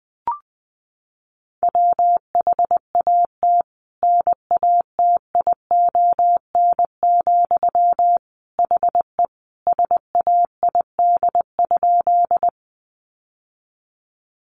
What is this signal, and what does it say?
Morse code, heard as a keyed tone, about 20 wpm, 700 Hz. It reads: WHAT NATION, HE SAID?
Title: Sentences from Top 900 Words 20wpm